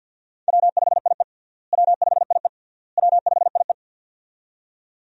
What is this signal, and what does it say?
Morse code, heard as a keyed tone, about 46 wpm, 700 Hz. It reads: W5IE W5IE W5IE
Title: USA Call Signs - 4 Characters - Repeated 3x - 50wpm